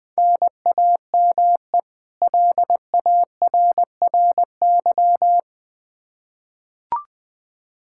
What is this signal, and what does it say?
Morse code, heard as a keyed tone, about 21 wpm, 700 Hz. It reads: NAME LARRY
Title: Practice QSO - Exchange 1 - Beginning - 20wpm